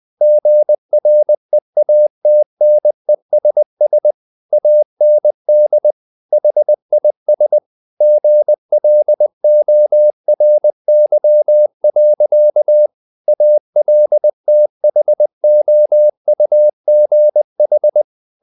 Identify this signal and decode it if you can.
Morse code, heard as a keyed tone, about 20 wpm, 600 Hz. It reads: GREATNESS AND HIS GLORY. ALTHOUGH
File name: the old man and the sea / the old man and the sea part 1 20wpm